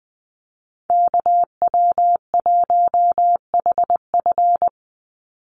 Morse code, heard as a keyed tone, about 20 wpm, 700 Hz. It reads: KW1HF
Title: USA Call Signs 20wpm